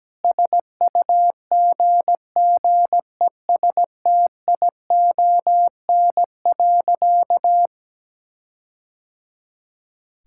Morse code, heard as a keyed tone, about 17 wpm, 700 Hz. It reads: SUGGESTION.
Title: Sentences from Top 1600 Words 17wpm